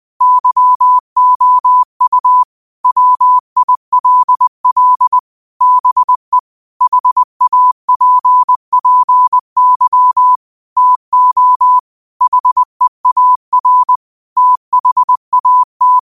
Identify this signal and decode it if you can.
Morse code, heard as a keyed tone, about 20 wpm, 1000 Hz. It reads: YOU WILL BE HAPPY TO HEAR THAT
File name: qotdfiles / 20wpm